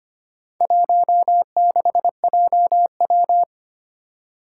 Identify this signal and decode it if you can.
Morse code, heard as a keyed tone, about 25 wpm, 700 Hz. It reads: 16JW
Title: Four Challenging Letter-Number Combinations 25wpm